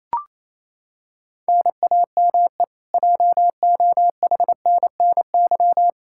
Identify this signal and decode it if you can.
Morse code, heard as a keyed tone, about 28 wpm, 700 Hz. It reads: NAME JOHNNY
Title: Practice QSO - Exchange 1 - Beginning - 28wpm